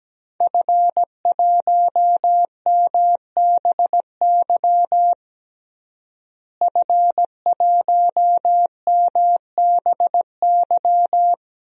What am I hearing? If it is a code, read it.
Morse code, heard as a keyed tone, about 17 wpm, 700 Hz. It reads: F1MBY F1MBY
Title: DXCC-100 Call Signs - 5 Characters - Repeated 3x - 17wpm